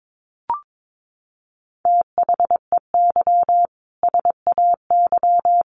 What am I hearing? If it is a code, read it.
Morse code, heard as a keyed tone, about 22 wpm, 700 Hz. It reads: THEY SAY
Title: Sentences from Top 1100 Words 22wpm